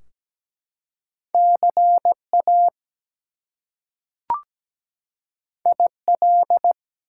Morse code, heard as a keyed tone, about 17 wpm, 700 Hz. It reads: CA IL
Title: US States and Canadian Provinces 17wpm